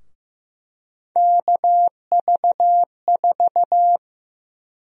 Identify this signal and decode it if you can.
Morse code, heard as a keyed tone, about 15 wpm, 700 Hz. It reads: KV4
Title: USA Call Sign Prefixes - 15wpm